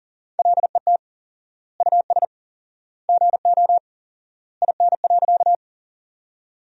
Morse code, heard as a keyed tone, about 40 wpm, 700 Hz. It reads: LET US GO IN.
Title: 4x Word-Spacing - Sentences from Top 300 Words - 40wpm